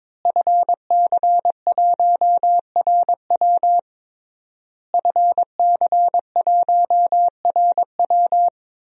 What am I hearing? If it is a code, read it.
Morse code, heard as a keyed tone, about 23 wpm, 700 Hz. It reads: FC1RW FC1RW
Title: DXCC-100 Call Signs - 5 Characters - Repeated 3x - 22wpm